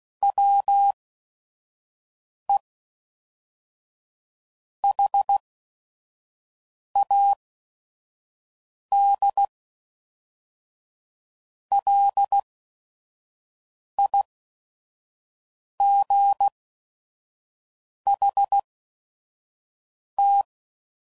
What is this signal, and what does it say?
Morse code, heard as a keyed tone, about 16 wpm, spaced as for 5 wpm, 800 Hz. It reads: WE HAD LIGHT
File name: CW Code Groups @5WPM / Weather Sets / Weather 4 28